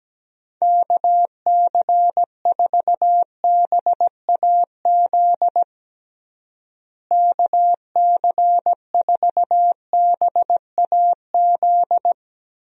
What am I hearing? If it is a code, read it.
Morse code, heard as a keyed tone, about 17 wpm, 700 Hz. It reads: KC4BAZ KC4BAZ
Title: USA Call Signs - 6 Characters - Repeated 3x - 17wpm